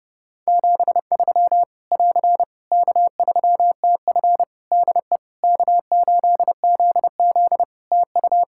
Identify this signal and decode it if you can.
Morse code, heard as a keyed tone, about 30 wpm, 700 Hz. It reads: 73 <AR> K3TF DE K8ZZ TU
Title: K1USN SST QSO Practice 30wpm